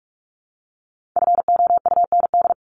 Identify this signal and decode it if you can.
Morse code, heard as a keyed tone, about 44 wpm, 700 Hz. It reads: FOUND
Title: Top 300 Words Repeated 45wpm